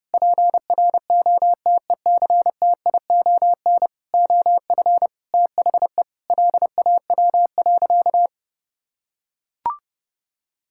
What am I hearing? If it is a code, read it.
Morse code, heard as a keyed tone, about 30 wpm, 700 Hz. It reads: PROTECTION OF THE LAW.
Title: Sentences from Top 1600 Words 30wpm